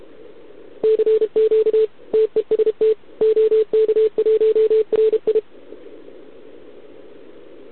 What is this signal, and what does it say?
Morse code, heard as a keyed tone, about 32 wpm, 440 Hz. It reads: CQ TEST OK1RI
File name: ARCHIV / OKDX2020 / OK1RIb